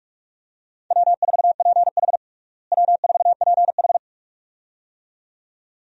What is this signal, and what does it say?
Morse code, heard as a keyed tone, about 46 wpm, 700 Hz. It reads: W4PH W4PH
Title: USA Call Signs - 4 Characters - Repeated 2x - 45wpm